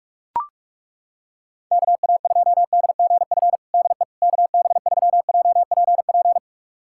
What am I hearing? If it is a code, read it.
Morse code, heard as a keyed tone, about 41 wpm, 700 Hz. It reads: KA2DGF DE KB3JPP
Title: USA Callsign DE Callsign - 6 Characters - 45wpm